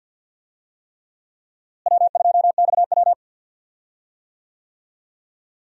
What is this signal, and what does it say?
Morse code, heard as a keyed tone, about 51 wpm, 700 Hz. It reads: W2XW
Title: USA Call Signs - 4 Characters - Repeated 3x - 50wpm